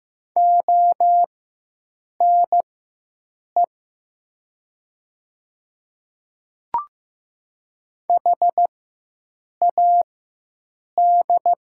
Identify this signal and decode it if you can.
Morse code, heard as a keyed tone, about 15 wpm, spaced as for 7 wpm, 700 Hz. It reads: ONE HAD
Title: 4x Character Spacing - Top 100 Words - 15wpm